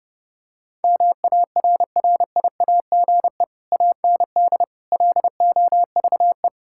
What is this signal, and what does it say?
Morse code, heard as a keyed tone, about 30 wpm, 700 Hz. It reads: MARRIAGE AND LOVE
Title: Sentences from Top 1000 Words - Review Part 5 - 30wpm